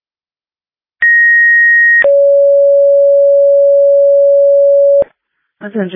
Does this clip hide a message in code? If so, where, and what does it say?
Yes: two-tone page from 1.0 to 5.0 s, heard as a one-second tone, then a three-second tone, about 1900 then 560 Hz, with speech.